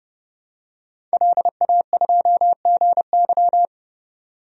Morse code, heard as a keyed tone, about 30 wpm, 700 Hz. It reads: LA2GY
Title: DXCC-100 Call Signs - 5 Characters - 30wpm